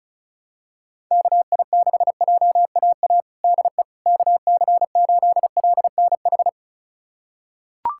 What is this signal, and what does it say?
Morse code, heard as a keyed tone, about 36 wpm, 700 Hz. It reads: KI6JAA DE KC8LNH
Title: USA Callsign DE Callsign 35wpm